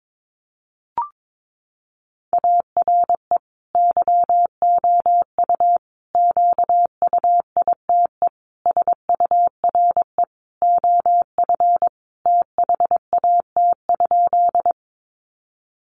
Morse code, heard as a keyed tone, about 22 wpm, 700 Hz. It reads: ARE YOU QUITE SURE OF THAT?
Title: Sentences from Top 700 Words 22wpm